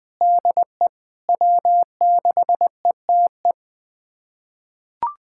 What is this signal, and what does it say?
Morse code, heard as a keyed tone, about 21 wpm, 700 Hz. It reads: DE W6ETE
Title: USA Callsign DE Callsign 20wpm